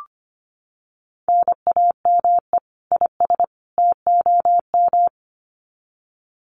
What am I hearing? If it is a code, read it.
Morse code, heard as a keyed tone, about 25 wpm, 700 Hz. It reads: NAME IS TOM
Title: Common QSO Phrases 25wpm